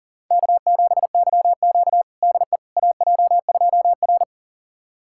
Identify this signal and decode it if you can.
Morse code, heard as a keyed tone, about 40 wpm, 700 Hz. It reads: K7YQ DE AJ2R
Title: USA Callsign DE Callsign - 4 Characters - 40wpm